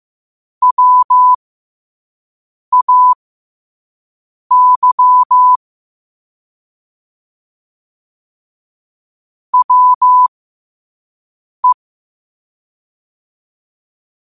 Morse code, heard as a keyed tone, about 15 wpm, spaced as for 5 wpm, 1000 Hz. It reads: WAY WE
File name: qotdfiles / 5wpm